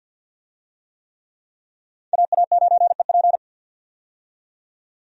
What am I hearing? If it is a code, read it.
Morse code, heard as a keyed tone, about 52 wpm, 700 Hz. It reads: AA9EP
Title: USA Call Signs - 5 Characters - 50wpm